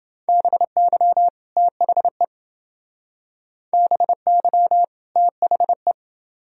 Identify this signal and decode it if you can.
Morse code, heard as a keyed tone, about 28 wpm, 700 Hz. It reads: BY THE BY THE